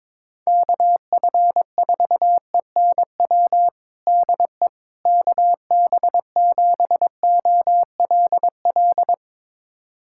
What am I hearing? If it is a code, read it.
Morse code, heard as a keyed tone, about 22 wpm, 700 Hz. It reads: KF4ENW DE KB7OLL
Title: USA Callsign DE Callsign - 6 Characters - 22wpm